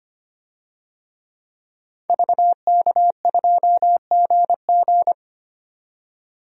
Morse code, heard as a keyed tone, about 25 wpm, 700 Hz. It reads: VK2GG
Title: DXCC-100 Call Signs - 5 Characters - 25wpm